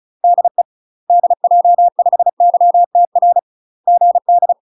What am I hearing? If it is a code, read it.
Morse code, heard as a keyed tone, about 37 wpm, 700 Hz. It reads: DE DJ5YTR GD
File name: UK MASTER QSOs TRAINING / 35WPM / UK MASTER QSOs 0001